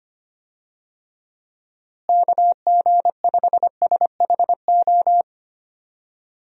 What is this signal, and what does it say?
Morse code, heard as a keyed tone, about 25 wpm, 700 Hz. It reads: KG5SHO
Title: USA Call Signs - 6 Characters - Repeated 2x - 25wpm